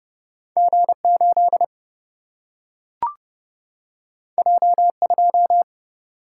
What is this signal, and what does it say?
Morse code, heard as a keyed tone, about 31 wpm, 700 Hz. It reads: G8 J2
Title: Single-Letter Single-Number 30wpm